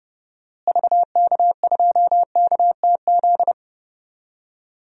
Morse code, heard as a keyed tone, about 30 wpm, 700 Hz. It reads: VK2KTZ
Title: DXCC-100 Call Signs - 6 Characters - Repeated 2x - 30wpm